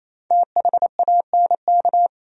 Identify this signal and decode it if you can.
Morse code, heard as a keyed tone, about 37 wpm, 700 Hz. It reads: THANK